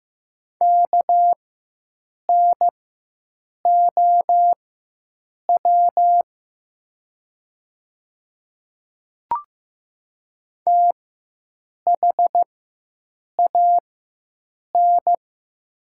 Morse code, heard as a keyed tone, about 15 wpm, spaced as for 7 wpm, 700 Hz. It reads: KNOW THAN
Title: 4x Character Spacing - Top 100 Words - 15wpm